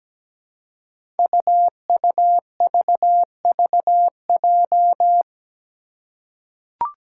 Morse code, heard as a keyed tone, about 17 wpm, 700 Hz. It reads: UUVVJ
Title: Five Challenging Letter-Number Combinations 17wpm